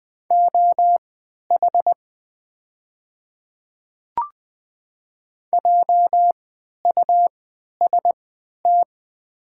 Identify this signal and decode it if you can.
Morse code, heard as a keyed tone, about 20 wpm, spaced as for 11 wpm, 700 Hz. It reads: OH JUST
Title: 3x Character Spacing - Top 500 Words - 20wpm